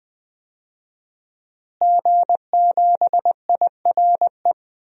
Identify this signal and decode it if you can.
Morse code, heard as a keyed tone, about 20 wpm, 700 Hz. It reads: G7IRE
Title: DXCC-100 Call Signs - 5 Characters - Repeated 2x - 20wpm